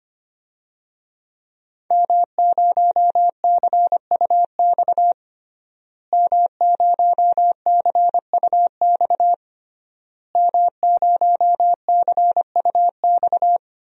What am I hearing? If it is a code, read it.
Morse code, heard as a keyed tone, about 25 wpm, 700 Hz. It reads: M0CUX M0CUX M0CUX
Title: DXCC-100 Call Signs - 5 Characters - Repeated 3x - 25wpm